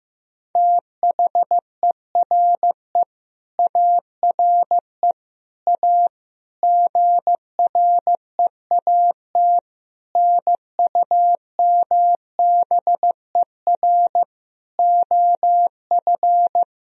Morse code, heard as a keyed tone, about 15 wpm, 700 Hz. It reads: THERE ARE A GREAT NUMBER OF
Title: Sentences from Top 1000 Words - Review Part 1 - 15wpm